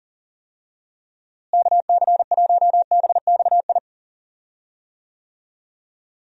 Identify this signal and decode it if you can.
Morse code, heard as a keyed tone, about 40 wpm, 700 Hz. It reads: KC1BXI
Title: USA Call Signs - 6 Characters - Repeated 2x - 40wpm